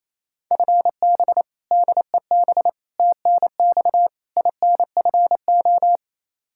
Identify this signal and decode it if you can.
Morse code, heard as a keyed tone, about 27 wpm, 700 Hz. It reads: FB DEB TNX INFO
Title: Practice QSO - Exchange 3 - More Info - 28wpm